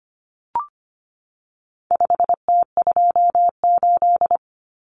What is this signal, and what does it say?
Morse code, heard as a keyed tone, about 25 wpm, 700 Hz. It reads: 5T28
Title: Common Cut Numbers - 4-Digit 25wpm